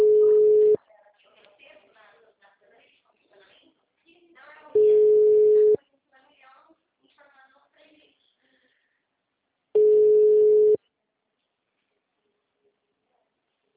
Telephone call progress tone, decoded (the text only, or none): ringback tone